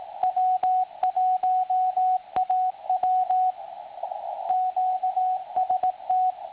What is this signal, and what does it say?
Morse code, heard as a keyed tone, about 18 wpm, 740 Hz, with radio noise. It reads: W1AW QST
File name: attachments / websdr recording 7047.5kHz